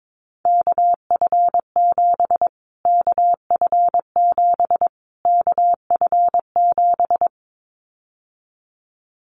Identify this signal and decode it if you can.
Morse code, heard as a keyed tone, about 22 wpm, 700 Hz. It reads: KF7 KF7 KF7
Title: Callsign Prefixes for Top 25 DXCC - Repeated 3x - 22wpm